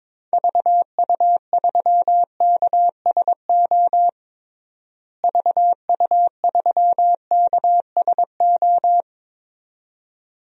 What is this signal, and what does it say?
Morse code, heard as a keyed tone, about 22 wpm, 700 Hz. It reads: VU3KSO VU3KSO
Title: DXCC-100 Call Signs - 6 Characters - Repeated 2x - 22wpm